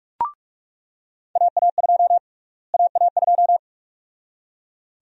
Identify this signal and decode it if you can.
Morse code, heard as a keyed tone, about 42 wpm, 700 Hz. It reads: AA2 AA2
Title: USA Call Sign Prefixes - Repeated 2x - 45wpm